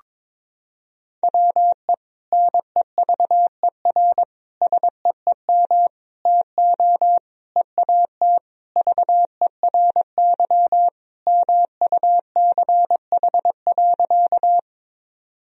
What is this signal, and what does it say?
Morse code, heard as a keyed tone, about 22 wpm, 700 Hz. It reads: WE NEVER SEEM TO EAT VERY MUCH.
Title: Sentences from Top 1000 Words - Review Part 1 - 22wpm